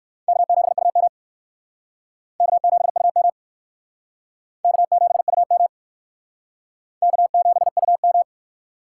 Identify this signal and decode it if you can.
Morse code, heard as a keyed tone, about 55 wpm, 700 Hz. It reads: K7UM K7UM K7UM K7UM
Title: Speed Racing - USA Call Signs 45wpm